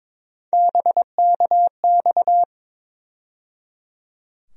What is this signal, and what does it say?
Morse code, heard as a keyed tone, about 22 wpm, 700 Hz. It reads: BKX
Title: USA Call Sign Postfixes - 3 Letters 22wpm